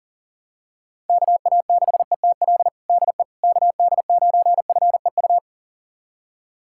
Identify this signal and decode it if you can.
Morse code, heard as a keyed tone, about 40 wpm, 700 Hz. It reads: KA6ETL DE KD9FEU
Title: USA Callsign DE Callsign - 6 Characters - 40wpm